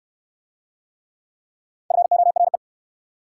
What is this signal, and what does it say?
Morse code, heard as a keyed tone, about 68 wpm, 700 Hz. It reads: FORE